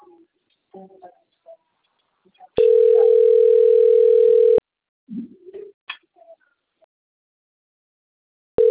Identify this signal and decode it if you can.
Ringback tone (North America), 440 + 480 Hz.